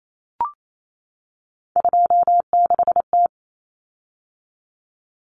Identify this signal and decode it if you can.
Morse code, heard as a keyed tone, about 28 wpm, 700 Hz. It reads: 26T
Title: Common Cut Numbers - 3-Digit 28wpm